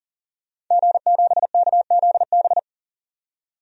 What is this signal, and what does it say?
Morse code, heard as a keyed tone, about 40 wpm, 700 Hz. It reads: G7KZB
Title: DXCC-100 Call Signs - 5 Characters - 40wpm